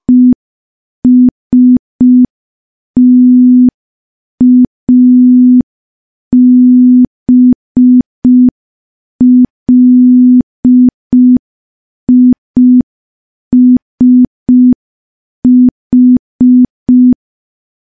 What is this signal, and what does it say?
Morse code, heard as a keyed tone, about 5 wpm, 250 Hz. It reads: ESTABLISH
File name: morse / morse-training-05WPM